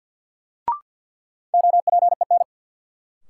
Morse code, heard as a keyed tone, about 48 wpm, 700 Hz. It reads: OPEN
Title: Sentences from Top 300 Words - New Word Drill - 50wpm